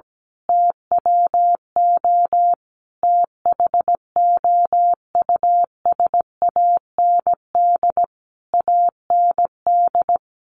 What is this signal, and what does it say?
Morse code, heard as a keyed tone, about 17 wpm, 700 Hz. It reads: TWO THOUSAND AND
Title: A Sound Of Thunder 17wpm